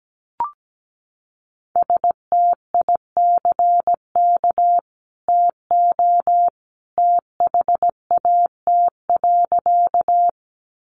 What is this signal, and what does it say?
Morse code, heard as a keyed tone, about 17 wpm, 700 Hz. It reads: STICK TO THAT.
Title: Sentences from Top 1200 Words 17wpm